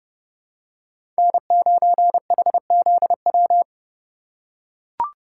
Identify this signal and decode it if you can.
Morse code, heard as a keyed tone, about 30 wpm, 700 Hz. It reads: N9HZW
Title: USA Call Signs - 5 Characters - 3x Character Spacing - 30wpm